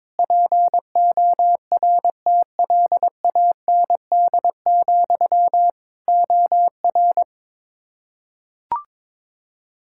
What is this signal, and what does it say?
Morse code, heard as a keyed tone, about 22 wpm, 700 Hz. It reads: PORTLAND, OR
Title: K1USN SST QSO Practice 22wpm